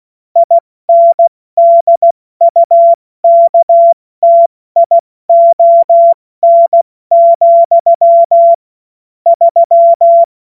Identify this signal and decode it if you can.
Morse code, heard as a keyed tone, about 16 wpm, spaced as for 14 wpm, 680 Hz. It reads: INDUKTION, 3
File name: Morsekurs / Witze / Witz 07. Dezember 2021, 70 F80